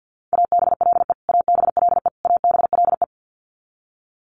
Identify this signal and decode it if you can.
Morse code, heard as a keyed tone, about 50 wpm, 700 Hz. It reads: ABLE ABLE ABLE
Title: Top 500 Words Repeated 50wpm